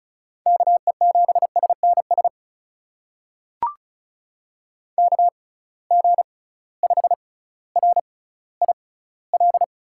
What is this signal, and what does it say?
Morse code, heard as a keyed tone, about 36 wpm, spaced as for 18 wpm, 700 Hz. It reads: KE7SNS KG5RIL
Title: USA Call Signs - 6 Characters - 6x Character Spacing - 35wpm